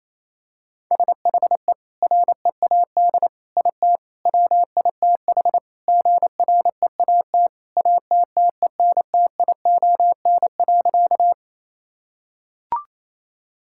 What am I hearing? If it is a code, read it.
Morse code, heard as a keyed tone, about 28 wpm, 700 Hz. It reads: SHE READ IT WITH GREAT ATTENTION.
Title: Sentences from Top 1000 Words - Review Part 6 - 28wpm